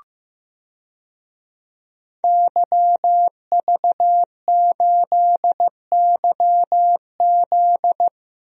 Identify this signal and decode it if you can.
Morse code, heard as a keyed tone, about 15 wpm, 700 Hz. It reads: YV8YZ
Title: DXCC-100 Call Signs - 5 Characters - 15wpm